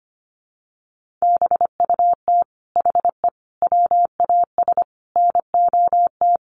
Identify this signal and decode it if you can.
Morse code, heard as a keyed tone, about 25 wpm, 700 Hz. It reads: BUT HE WAS NOT
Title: Sentences from Top 1500 Words 25wpm